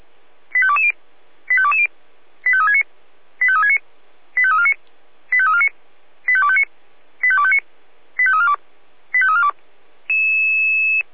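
Five-tone selective call (ZVEI), heard as quick runs of five tones, tones of about 70 ms. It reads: 85220 85220 85369 85369 85349 85349 85259 85259 85342 85342
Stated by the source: ZVEI five-tone call 85630 received by SDR.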